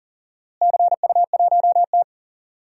Morse code, heard as a keyed tone, about 40 wpm, 700 Hz. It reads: CU1T
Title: DXCC-100 Call Signs - 4 Characters - Repeated 3x - 40wpm